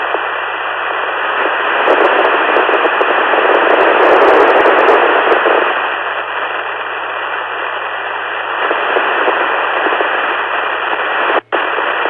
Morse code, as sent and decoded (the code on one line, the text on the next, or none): none